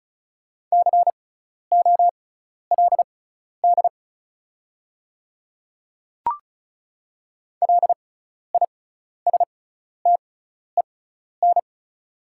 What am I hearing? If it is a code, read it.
Morse code, heard as a keyed tone, about 36 wpm, spaced as for 12 wpm, 690 Hz. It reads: COLD LISTEN
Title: 6x Character Spacing - Top 500 Words - 35wpm